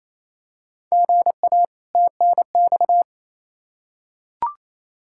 Morse code, heard as a keyed tone, about 28 wpm, 700 Hz. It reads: GA TNX
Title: Practice QSO - Exchange 1 - Beginning - 28wpm